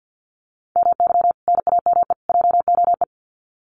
Morse code, heard as a keyed tone, about 50 wpm, 700 Hz. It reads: MY NAME JOE
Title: Common QSO Phrases 50wpm